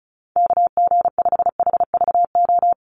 Morse code, heard as a keyed tone, about 36 wpm, 700 Hz. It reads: KG5HVO
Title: Call Signs - 35wpm